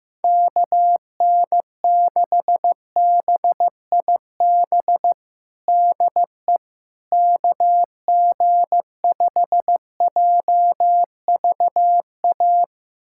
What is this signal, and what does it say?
Morse code, heard as a keyed tone, about 15 wpm, 700 Hz. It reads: KN6BIB DE KG5JVA
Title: USA Callsign DE Callsign - 6 Characters - 15wpm